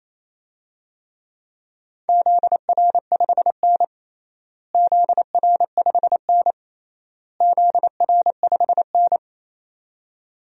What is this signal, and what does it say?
Morse code, heard as a keyed tone, about 27 wpm, 700 Hz. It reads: ZR5N ZR5N ZR5N
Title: DXCC-100 Call Signs - 4 Characters - Repeated 3x - 28wpm